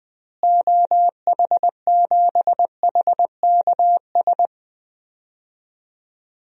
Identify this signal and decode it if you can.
Morse code, heard as a keyed tone, about 20 wpm, 700 Hz. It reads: OH7HKS
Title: DXCC-100 Call Signs - 6 Characters - Repeated 3x - 20wpm